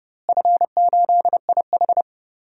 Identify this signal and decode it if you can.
Morse code, heard as a keyed tone, about 30 wpm, 700 Hz. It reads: F8IH
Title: DXCC-100 Call Signs - 4 Characters - Repeated 2x - 30wpm